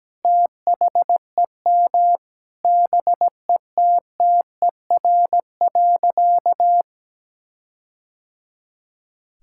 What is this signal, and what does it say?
Morse code, heard as a keyed tone, about 17 wpm, 700 Hz. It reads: THEM BETTER.
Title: Sentences from Top 1600 Words 17wpm